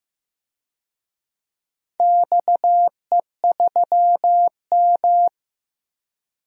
Morse code, heard as a keyed tone, about 15 wpm, 700 Hz. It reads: XE3M
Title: DXCC-100 Call Signs - 4 Characters - 15wpm